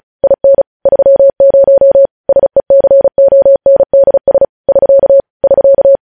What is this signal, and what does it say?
Morse code, heard as a keyed tone, about 36 wpm, 550 Hz. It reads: IN 30 SECONDS <SK> <SK>
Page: news headlines in morse code at 35 wpm